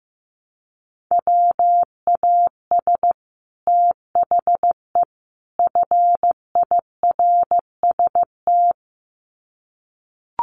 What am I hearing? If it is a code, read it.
Morse code, heard as a keyed tone, about 15 wpm, 700 Hz. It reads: WAS THE FIRST
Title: Sets of 3 Words 15wpm